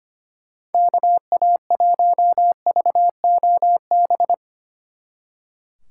Morse code, heard as a keyed tone, about 33 wpm, 700 Hz. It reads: KA1VOB